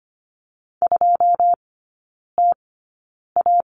Morse code, heard as a keyed tone, about 25 wpm, spaced as for 9 wpm, 690 Hz. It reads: 2TA
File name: Lesson 14e - Number 2 - Three Characters 25wpm